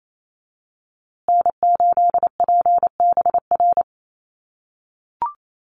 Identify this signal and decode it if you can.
Morse code, heard as a keyed tone, about 28 wpm, 700 Hz. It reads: N8PBR
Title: USA Call Signs Repeated 2x 28wpm